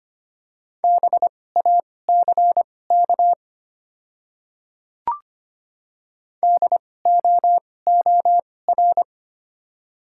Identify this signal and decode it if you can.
Morse code, heard as a keyed tone, about 25 wpm, spaced as for 18 wpm, 690 Hz. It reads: BACK DOOR
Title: 2x Character Spacing - Top 500 Words - 25wpm